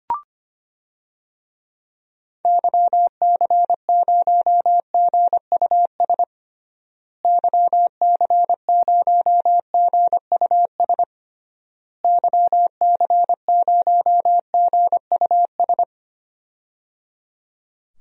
Morse code, heard as a keyed tone, about 25 wpm, 700 Hz. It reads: YC0GUS YC0GUS YC0GUS
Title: DXCC-100 Call Signs - 6 Characters - Repeated 3x - 25wpm